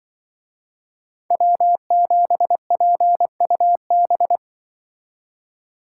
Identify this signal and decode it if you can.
Morse code, heard as a keyed tone, about 24 wpm, 700 Hz. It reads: W7PUB
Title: Speed Racing - USA 1x3 Call Signs 20wpm